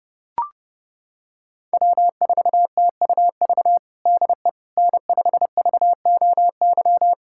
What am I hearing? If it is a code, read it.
Morse code, heard as a keyed tone, about 30 wpm, 700 Hz. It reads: W4TUV DE N5VOY
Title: USA Callsign DE Callsign - 5 Characters - 30wpm